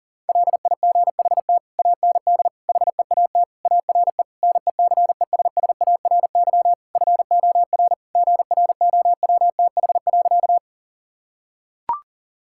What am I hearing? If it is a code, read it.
Morse code, heard as a keyed tone, about 40 wpm, 700 Hz. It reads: LIGHT AND HEAT ARE NECESSARY FOR GROWTH.
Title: Sentences from Top 1000 Words - Review Part 5 - 40wpm